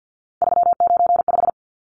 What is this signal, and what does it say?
Morse code, heard as a keyed tone, about 49 wpm, 700 Hz. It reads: 395
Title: Numbers - 3-Digit 50wpm